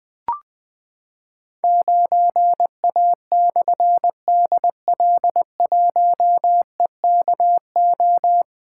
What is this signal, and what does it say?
Morse code, heard as a keyed tone, about 20 wpm, 700 Hz. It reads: 9A/DL1EKO
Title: Difficult Call Signs 20wpm